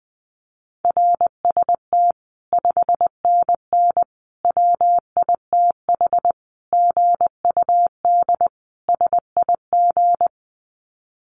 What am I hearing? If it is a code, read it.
Morse code, heard as a keyed tone, about 20 wpm, 700 Hz. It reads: RST 5NN WITH GUD SIG
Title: Common QSO Phrases 20wpm